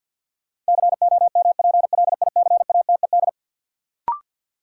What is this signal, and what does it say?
Morse code, heard as a keyed tone, about 52 wpm, 700 Hz. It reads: COMPLICATED
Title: Sentences from Top 1200 Words - New Word Drill - 50wpm